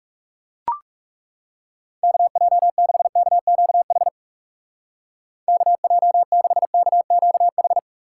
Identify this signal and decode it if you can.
Morse code, heard as a keyed tone, about 38 wpm, 700 Hz. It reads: KJ6KQH KJ6KQH